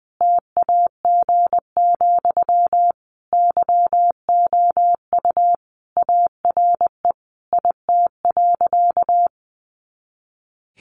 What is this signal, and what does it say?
Morse code, heard as a keyed tone, about 20 wpm, 700 Hz. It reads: TAG, YOU ARE IT.
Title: Additional Easy Phrases 20wpm